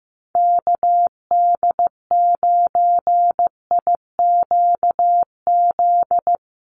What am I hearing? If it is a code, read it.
Morse code, heard as a keyed tone, about 15 wpm, 700 Hz. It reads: KD9IQZ
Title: USA Call Signs Repeated 2x 15wpm